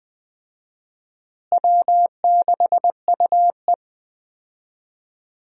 Morse code, heard as a keyed tone, about 20 wpm, 700 Hz. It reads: W6UE
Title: USA Call Signs - 4 Characters - Repeated 3x - 20wpm